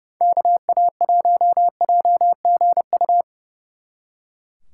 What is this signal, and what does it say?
Morse code, heard as a keyed tone, about 39 wpm, 700 Hz. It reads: KA1JGU